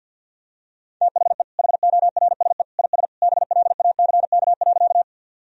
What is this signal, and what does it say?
Morse code, heard as a keyed tone, about 50 wpm, 700 Hz. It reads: THE HORSE IS BLACK.
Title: Sentences from Top 1000 Words - Review Part 2 - 50wpm